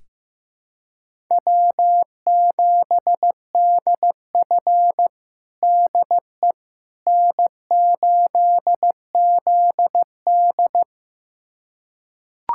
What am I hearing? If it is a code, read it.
Morse code, heard as a keyed tone, about 16 wpm, 700 Hz. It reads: W7DF DE N8ZD
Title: USA Callsign DE Callsign - 4 Characters - 15wpm